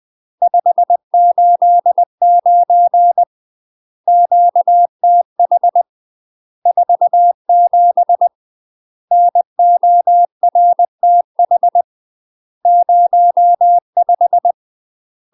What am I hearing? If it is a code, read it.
Morse code, heard as a keyed tone, about 20 wpm, 700 Hz. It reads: 589 QTH 47 NORTH 05
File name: UK MASTER QSOs TRAINING / 20WPM / UK MASTER QSOs 0037